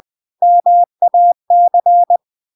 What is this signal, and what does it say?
Morse code, heard as a keyed tone, about 20 wpm, 700 Hz. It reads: MAC
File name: recordings / 20wpm0000